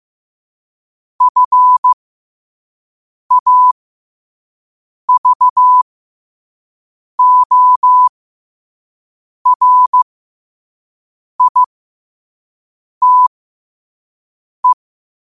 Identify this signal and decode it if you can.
Morse code, heard as a keyed tone, about 15 wpm, spaced as for 5 wpm, 1000 Hz. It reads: FAVORITE
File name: qotdfiles / 5wpm